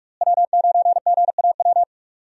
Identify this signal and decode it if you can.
Morse code, heard as a keyed tone, about 45 wpm, 700 Hz. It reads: W9GAW
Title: USA Call Signs - 5 Characters - Repeated 2x - 45wpm